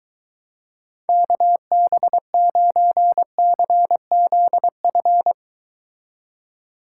Morse code, heard as a keyed tone, about 23 wpm, 700 Hz. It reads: KB9CZF